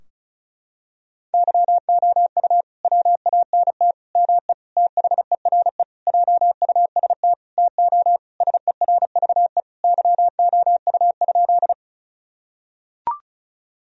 Morse code, heard as a keyed tone, about 36 wpm, 700 Hz. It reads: YOU WANT ME THERE JUST TO SERVE YOU?
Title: Sentences from Top 1000 Words - Review Part 2 - 35wpm